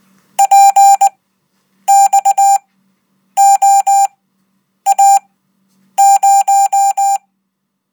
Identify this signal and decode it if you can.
Morse code, heard as a keyed tone, about 19 wpm, spaced as for 8 wpm, 790 Hz. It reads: PXOA0
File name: Code